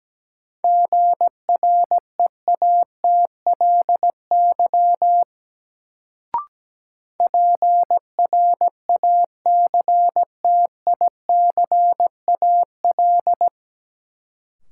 Morse code, heard as a keyed tone, about 17 wpm, 700 Hz. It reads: GREATLY PRACTICAL
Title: Sentences from Top 1400 Words - New Word Drill - 17wpm